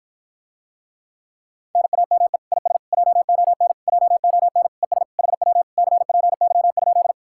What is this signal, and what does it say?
Morse code, heard as a keyed tone, about 54 wpm, 700 Hz. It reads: NAME IS JON JON ES HW CPY?